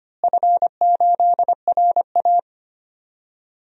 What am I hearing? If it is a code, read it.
Morse code, heard as a keyed tone, about 25 wpm, 700 Hz. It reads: F8RA
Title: DXCC-100 Call Signs - 4 Characters - Repeated 3x - 25wpm